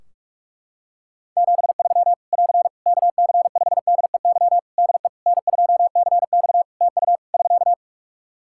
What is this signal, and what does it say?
Morse code, heard as a keyed tone, about 43 wpm, 700 Hz. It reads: 73 <AR> KC5DEY DE N2CX TU <SK>
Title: Practice QSO - Exchange 4 - Ending - 45wpm